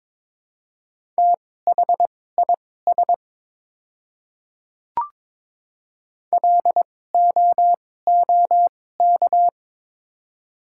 Morse code, heard as a keyed tone, about 22 wpm, spaced as for 16 wpm, 700 Hz. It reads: THIS LOOK
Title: 2x Character Spacing - Top 100 Words - 22wpm